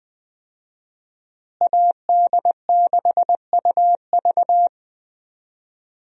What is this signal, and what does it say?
Morse code, heard as a keyed tone, about 20 wpm, 700 Hz. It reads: AD6UV
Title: USA Call Signs - 5 Characters - Repeated 3x - 20wpm